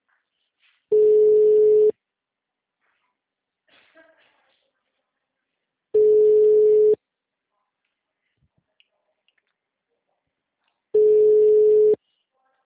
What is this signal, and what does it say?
Ringback tone (Europe, 425 Hz), with speech.